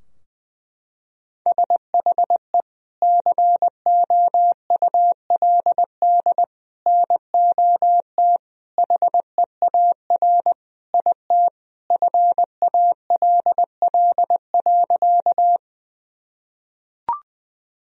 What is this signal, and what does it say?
Morse code, heard as a keyed tone, about 20 wpm, 700 Hz. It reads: SHE COULD NOT HEAR IT FALL.
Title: Sentences from Top 1000 Words - Review Part 2 - 20wpm